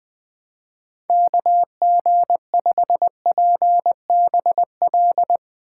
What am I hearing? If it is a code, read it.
Morse code, heard as a keyed tone, about 20 wpm, 700 Hz. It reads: KG5PBL
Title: Speed Racing - USA 2x3 Call Signs 17wpm